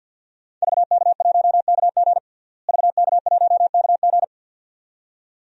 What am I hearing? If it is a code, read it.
Morse code, heard as a keyed tone, about 49 wpm, 700 Hz. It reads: VK1KG VK1KG
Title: DXCC-100 Call Signs - 5 Characters - Repeated 2x - 50wpm